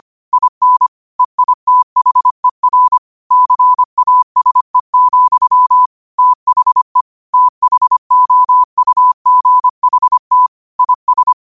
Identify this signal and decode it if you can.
Morse code, heard as a keyed tone, about 25 wpm, 1000 Hz. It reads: IN EITHER CASE, THE THOUGHT IS
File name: qotdfiles / 25wpm